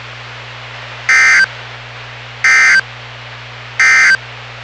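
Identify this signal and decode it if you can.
Emergency Alert System (SAME) tones, heard as harsh buzzing bursts. It is heard from 1.0 to 4.2 s.
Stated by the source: NOAA Weather Radio station WWG57 on 162.450 MHz, local weather broadcast.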